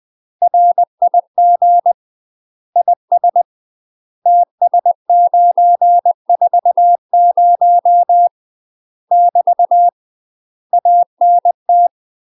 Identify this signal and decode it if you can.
Morse code, heard as a keyed tone, about 20 wpm, 700 Hz. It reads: RIG IS TS940 <BT> ANT
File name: UK MASTER QSOs TRAINING / 20WPM / UK MASTER QSOs 0043